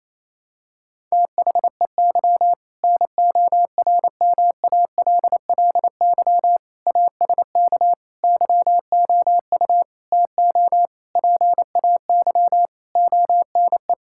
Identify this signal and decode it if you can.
Morse code, heard as a keyed tone, about 28 wpm, 700 Hz. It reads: THEY NORMALLY ASK YOU TO PAY ONE
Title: Sentences from Top 1000 Words - Review Part 6 - 28wpm